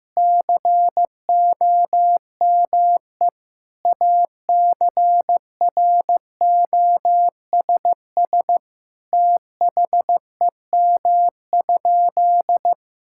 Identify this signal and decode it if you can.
Morse code, heard as a keyed tone, about 15 wpm, 700 Hz. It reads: COME ACROSS THEM?
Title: Sentences from Top 1000 Words - Review Part 5 - 15wpm